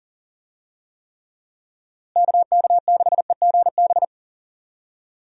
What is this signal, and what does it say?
Morse code, heard as a keyed tone, about 37 wpm, 700 Hz. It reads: KK6EGB
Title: USA Call Signs - 6 Characters - Repeated 3x - 40wpm